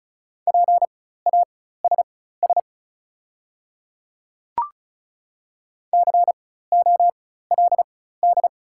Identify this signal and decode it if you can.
Morse code, heard as a keyed tone, about 34 wpm, spaced as for 16 wpm, 690 Hz. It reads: PASS COLD
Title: 4x Character Spacing - Top 500 Words - 35wpm